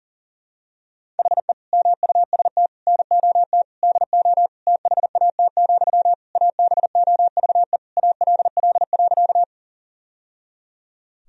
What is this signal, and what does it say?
Morse code, heard as a keyed tone, about 37 wpm, 700 Hz. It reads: HE MUST NOT DO THAT, ABOVE ALL.
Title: Sentences from Top 1000 Words - Review Part 2 - 40wpm